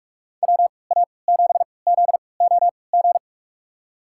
Morse code, heard as a keyed tone, about 47 wpm, spaced as for 26 wpm, 700 Hz. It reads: WA7ZOG
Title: USA Call Signs - 3x Character Spacing - 45wpm